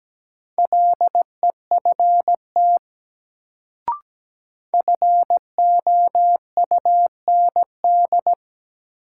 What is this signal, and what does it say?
Morse code, heard as a keyed tone, about 17 wpm, 700 Hz. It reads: LEFT FOUND
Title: Sentences from Top 300 Words - New Word Drill - 17wpm